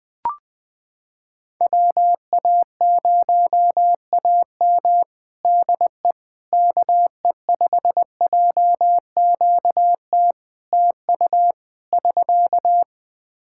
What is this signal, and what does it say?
Morse code, heard as a keyed tone, about 20 wpm, 700 Hz. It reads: WA0AM DE KE5JQT TU <SK>
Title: Practice QSO - Exchange 4 - Ending - 20wpm